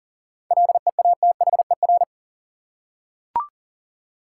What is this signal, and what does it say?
Morse code, heard as a keyed tone, about 40 wpm, 700 Hz. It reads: LEATHER
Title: Seven Letter Words - 40wpm